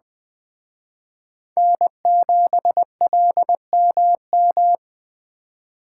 Morse code, heard as a keyed tone, about 20 wpm, 700 Hz. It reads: N7LMM
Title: Speed Racing - USA 1x3 Call Signs 17wpm